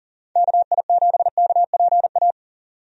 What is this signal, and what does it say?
Morse code, heard as a keyed tone, about 40 wpm, 700 Hz. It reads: KI7KPA
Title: USA Call Signs - 6 Characters - 40wpm